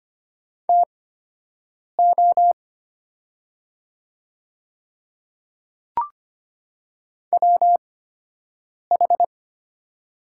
Morse code, heard as a keyed tone, about 25 wpm, spaced as for 7 wpm, 700 Hz. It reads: TO WH
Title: 8x Character Spacing - Top 300 Words - 25wpm